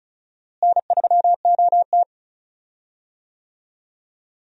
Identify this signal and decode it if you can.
Morse code, heard as a keyed tone, about 36 wpm, 700 Hz. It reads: N3OT
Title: USA Call Signs - 4 Characters - Repeated 3x - 35wpm